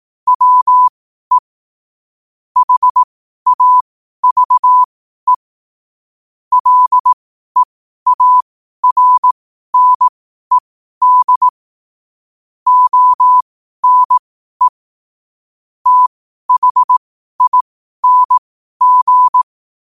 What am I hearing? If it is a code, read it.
Morse code, heard as a keyed tone, about 18 wpm, spaced as for 12 wpm, 1000 Hz. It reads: WE HAVE LEARNED ONE THING
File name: qotdfiles / 12wpm